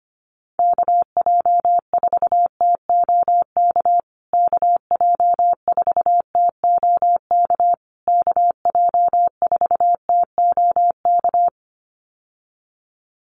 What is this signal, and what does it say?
Morse code, heard as a keyed tone, about 25 wpm, 700 Hz. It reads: KJ4TOK KJ4TOK KJ4TOK
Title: USA Call Signs Repeated 3x 25wpm